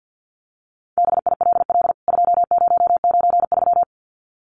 Morse code, heard as a keyed tone, about 49 wpm, 700 Hz. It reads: BILL 2093
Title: CWT Exchange 50wpm